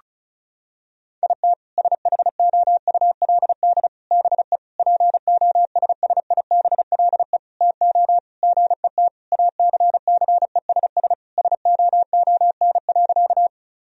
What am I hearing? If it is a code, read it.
Morse code, heard as a keyed tone, about 34 wpm, 700 Hz. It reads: IT SHOULD BE POSSIBLE TO GET ACCESS SOON.
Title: Sentences from Top 1000 Words - Review Part 4 - 35wpm